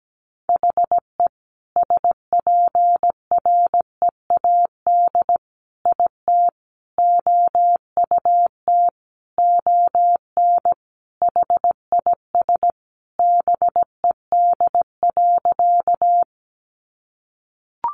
Morse code, heard as a keyed tone, about 17 wpm, 700 Hz. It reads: HE SPREAD IT OUT ON HIS BED.
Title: Sentences from Top 1200 Words 17wpm